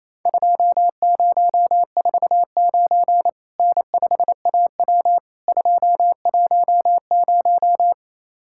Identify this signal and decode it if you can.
Morse code, heard as a keyed tone, about 28 wpm, 700 Hz. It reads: 2049 N5AW 210
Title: Worked All Europe DX Contest - QTC Exchange - 28wpm